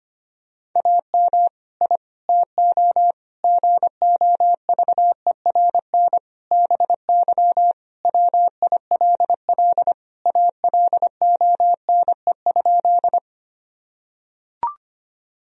Morse code, heard as a keyed tone, about 25 wpm, 700 Hz. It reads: AM I TO GOVERN BY WILL ALONE?
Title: Sentences from Top 1000 Words - Review Part 2 - 25wpm